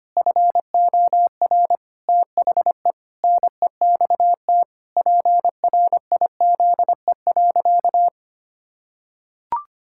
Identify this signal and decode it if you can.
Morse code, heard as a keyed tone, about 25 wpm, 700 Hz. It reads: FOR THE NEXT PRIZE.
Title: Pangram Challenge v2 - 25wpm